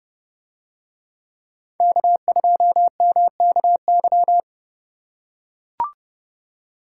Morse code, heard as a keyed tone, about 29 wpm, 700 Hz. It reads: K2MKY